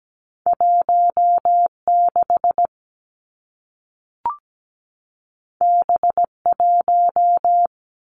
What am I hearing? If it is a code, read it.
Morse code, heard as a keyed tone, about 17 wpm, 700 Hz. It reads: 16 B1
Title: Two Challenging Letter-Number Combinations 17wpm